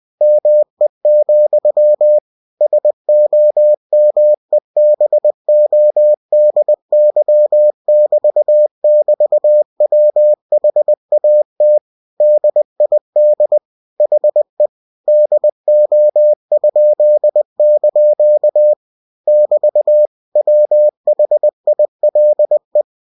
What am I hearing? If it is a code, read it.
Morse code, heard as a keyed tone, about 20 wpm, 600 Hz. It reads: ME, SOMEBODY<BT><BT>WHAT DID HE DO?) <BT> WHILE